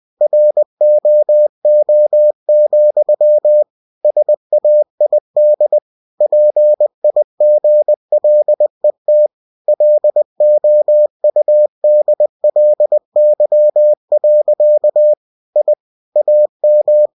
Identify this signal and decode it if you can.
Morse code, heard as a keyed tone, about 20 wpm, 600 Hz. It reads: ROO, SAID PIGLET LOUDLY. I AM
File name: cw-books / winnie-the-pooh / winnie-20wpm-page-0072